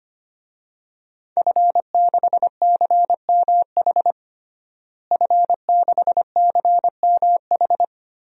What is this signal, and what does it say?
Morse code, heard as a keyed tone, about 25 wpm, 700 Hz. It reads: F6CMH F6CMH
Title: DXCC-100 Call Signs - 5 Characters - Repeated 3x - 25wpm